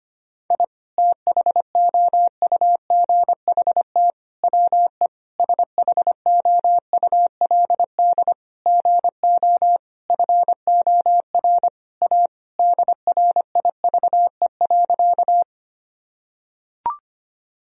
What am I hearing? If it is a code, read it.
Morse code, heard as a keyed tone, about 25 wpm, 700 Hz. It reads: I THOUGHT WE SHOULD GO FOR A DRIVE.
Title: Sentences from Top 1000 Words - Review Part 2 - 25wpm